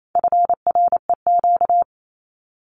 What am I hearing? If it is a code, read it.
Morse code, heard as a keyed tone, about 28 wpm, 700 Hz. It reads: FREQ
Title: Abbreviations Repeated 3x - 28wpm